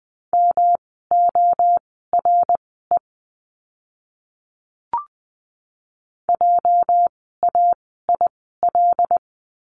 Morse code, heard as a keyed tone, about 20 wpm, spaced as for 15 wpm, 700 Hz. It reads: MORE JAIL
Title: 206 Lesson 33f - Letter j - Words 20wpm